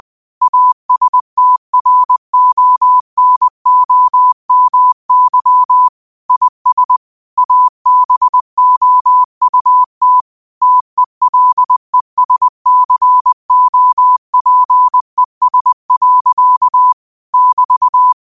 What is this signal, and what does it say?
Morse code, heard as a keyed tone, about 20 wpm, 1000 Hz. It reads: ASTRONOMY IS ABOUT TELESCOPES. <BT>